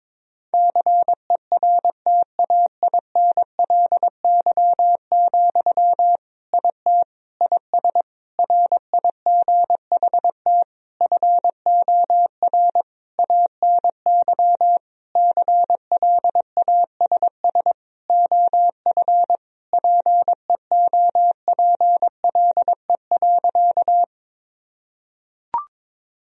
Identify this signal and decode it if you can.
Morse code, heard as a keyed tone, about 21 wpm, 700 Hz. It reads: CERTAINLY, IT IS RIGHT FOR ANY CLASS OF PEOPLE.
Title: Sentences from Top 1000 Words - Review Part 2 - 22wpm